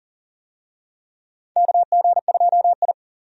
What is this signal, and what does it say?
Morse code, heard as a keyed tone, about 40 wpm, 700 Hz. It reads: KG2I
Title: USA Call Signs - 4 Characters - 40wpm